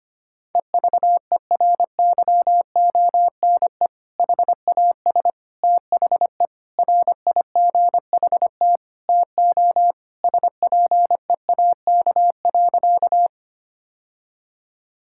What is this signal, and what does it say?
Morse code, heard as a keyed tone, about 25 wpm, 700 Hz. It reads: EVERYONE HAS THE RIGHT TO SPEAK.
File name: practice / 25wpm / Sentences with 3-Word Phrases - 25wpm